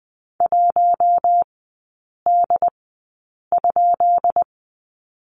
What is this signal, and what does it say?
Morse code, heard as a keyed tone, about 20 wpm, spaced as for 8 wpm, 690 Hz. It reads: 1D?
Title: 133 Lesson 22e - Question Mark - Three Characters 20wpm